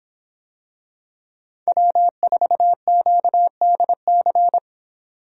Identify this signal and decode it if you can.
Morse code, heard as a keyed tone, about 26 wpm, 700 Hz. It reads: W4QDC